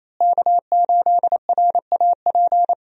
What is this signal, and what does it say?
Morse code, heard as a keyed tone, about 28 wpm, 700 Hz. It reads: K8RAP
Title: USA Call Signs - 5 Characters - Repeated 3x - 28wpm